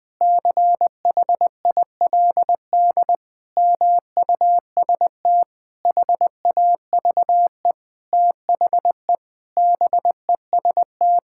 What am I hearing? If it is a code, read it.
Morse code, heard as a keyed tone, about 21 wpm, 700 Hz. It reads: CHILD MUST HAVE THE BEST
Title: Sentences from Top 1000 Words - Review Part 6 - 20wpm